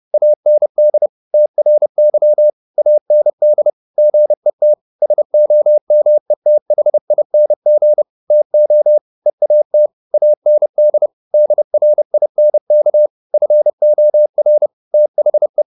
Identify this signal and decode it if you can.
Morse code, heard as a keyed tone, about 30 wpm, 600 Hz. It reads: AND TRY AND GET SOMETHING TO EAT AND DRINK FOR THE